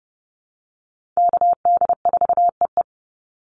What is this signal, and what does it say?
Morse code, heard as a keyed tone, about 30 wpm, 700 Hz. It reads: KD4EE
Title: Call Signs - 30wpm